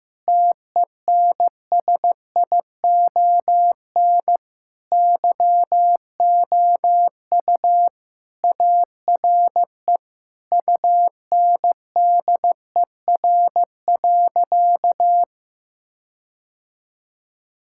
Morse code, heard as a keyed tone, about 15 wpm, 700 Hz. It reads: TENSION YOU ARE UNDER.
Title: Sentences from Top 1600 Words 15wpm